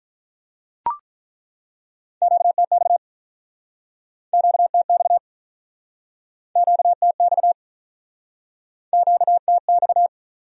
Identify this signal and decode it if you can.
Morse code, heard as a keyed tone, about 49 wpm, 700 Hz. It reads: QTX QTX QTX QTX